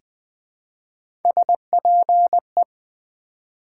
Morse code, heard as a keyed tone, about 20 wpm, 700 Hz. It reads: SPE
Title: Build A Word - 1 Letter At A Time - 5-Letter Words - 20wpm